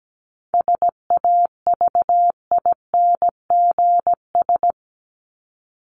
Morse code, heard as a keyed tone, about 17 wpm, 700 Hz. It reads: SAVINGS
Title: Top 1001-1250 Words 17wpm